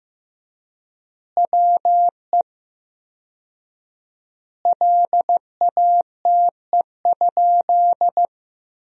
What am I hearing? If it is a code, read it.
Morse code, heard as a keyed tone, about 15 wpm, 700 Hz. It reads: WE LATE?
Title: 4x Word-Spacing - Sentences from Top 300 Words - 15wpm